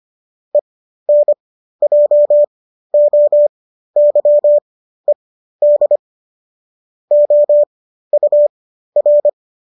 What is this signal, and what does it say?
Morse code, heard as a keyed tone, about 25 wpm, spaced as for 13 wpm, 590 Hz. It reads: ENJOYED OUR